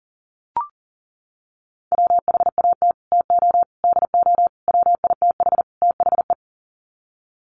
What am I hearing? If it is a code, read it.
Morse code, heard as a keyed tone, about 37 wpm, 700 Hz. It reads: WHAT TO DO WITH THE
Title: Sets of 5 Words Encore2 40wpm